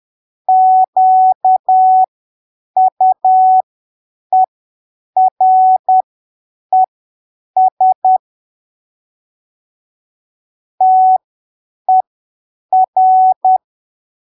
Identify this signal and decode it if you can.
Morse code, heard as a keyed tone, about 10 wpm, spaced as for 7 wpm, 750 Hz. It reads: QUERES TER
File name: arquivos / texto / texto 26 10ppm